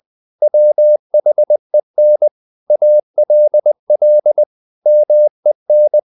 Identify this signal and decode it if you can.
Morse code, heard as a keyed tone, about 20 wpm, 600 Hz. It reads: WHEN ALL MEN
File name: CW / Practice / If20